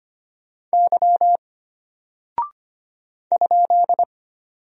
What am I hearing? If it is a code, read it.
Morse code, heard as a keyed tone, about 25 wpm, 700 Hz. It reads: Y ?
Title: Single Character (Letter:Number:Punctuation:Prosign) Alt 25wpm